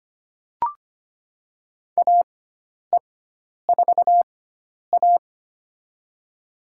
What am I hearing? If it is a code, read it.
Morse code, heard as a keyed tone, about 25 wpm, spaced as for 10 wpm, 700 Hz. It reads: AE4A
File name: Lesson 16g - Letter c - Call Signs 25wpm